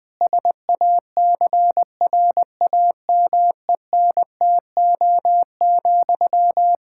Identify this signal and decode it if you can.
Morse code, heard as a keyed tone, about 20 wpm, 700 Hz. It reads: SACRAMENTO,
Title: Practice QSO - Exchange 1 - Beginning - 20wpm